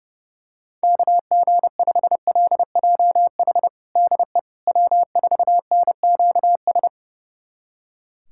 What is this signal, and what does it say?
Morse code, heard as a keyed tone, about 30 wpm, 700 Hz. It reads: KG5LJH DE W4NQS
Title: USA Callsign DE Callsign 30wpm